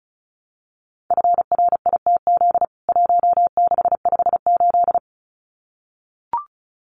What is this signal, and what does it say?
Morse code, heard as a keyed tone, about 34 wpm, 700 Hz. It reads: FRITZ 1658
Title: CWT Exchange 35wpm